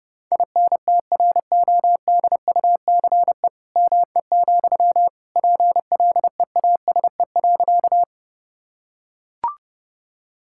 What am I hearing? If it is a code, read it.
Morse code, heard as a keyed tone, about 30 wpm, 700 Hz. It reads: INTRODUCE ME, PLEASE.
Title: Sentences from Top 1600 Words 30wpm